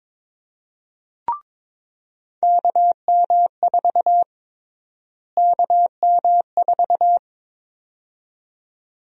Morse code, heard as a keyed tone, about 22 wpm, 700 Hz. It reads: KM4 KM4
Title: USA Call Sign Prefixes - Repeated 2x - 22wpm